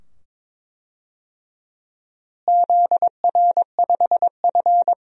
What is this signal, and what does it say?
Morse code, heard as a keyed tone, about 21 wpm, 700 Hz. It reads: ZR5F
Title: DXCC-100 Call Signs - 4 Characters - Repeated 2x - 22wpm